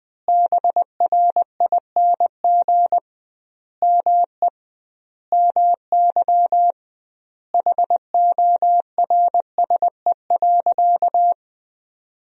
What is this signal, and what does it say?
Morse code, heard as a keyed tone, about 20 wpm, 700 Hz. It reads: BRING ME MY HORSE.
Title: 2x Word-Spacing - Sentences from Top 500 Words - 20wpm